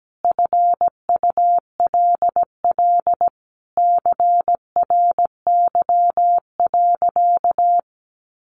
Morse code, heard as a keyed tone, about 17 wpm, 700 Hz. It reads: FULL CRY.
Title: Sentences from Top 1500 Words 17wpm